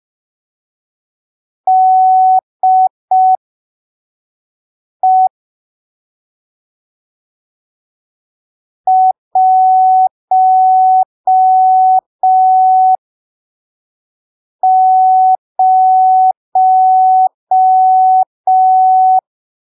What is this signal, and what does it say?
Morse code, heard as a keyed tone, about 5 wpm, spaced as for 3 wpm, 750 Hz. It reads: DE 10
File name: arquivos / texto / texto 12 05ppm